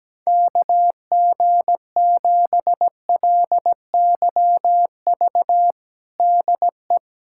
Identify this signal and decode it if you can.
Morse code, heard as a keyed tone, about 17 wpm, 700 Hz. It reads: KG7LYV DE
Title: USA Callsign DE Callsign - 6 Characters - 17wpm